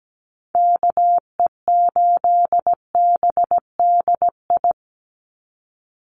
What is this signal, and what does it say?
Morse code, heard as a keyed tone, about 17 wpm, 700 Hz. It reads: KE8BDI
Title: USA Call Signs 17wpm